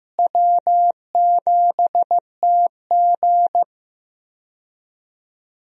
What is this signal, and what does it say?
Morse code, heard as a keyed tone, about 15 wpm, 700 Hz. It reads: W7TG
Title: USA Call Signs - 4 Characters - Repeated 2x - 15wpm